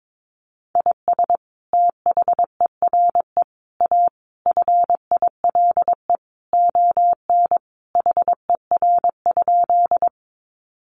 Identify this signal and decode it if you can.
Morse code, heard as a keyed tone, about 22 wpm, 700 Hz. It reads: IS THERE A FILE ON HER?
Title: Sentences from Top 1200 Words 22wpm